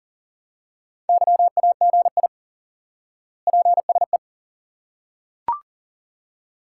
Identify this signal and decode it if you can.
Morse code, heard as a keyed tone, about 37 wpm, 700 Hz. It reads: YAGI PSE
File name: IQE Course 3 - Lesson 89 - YAGI - Two Elements - 40wpm